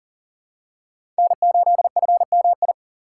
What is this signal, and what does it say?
Morse code, heard as a keyed tone, about 40 wpm, 700 Hz. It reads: N8FMI
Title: USA Call Signs - 5 Characters - 6x Character Spacing - 40wpm